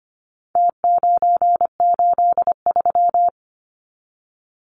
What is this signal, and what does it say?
Morse code, heard as a keyed tone, about 25 wpm, 700 Hz. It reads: T983
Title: Numbers - Contest Style - 4-Digit 25wpm